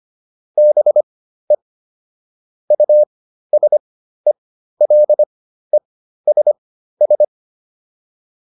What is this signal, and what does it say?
Morse code, heard as a keyed tone, about 25 wpm, spaced as for 13 wpm, 600 Hz. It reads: BE USELESS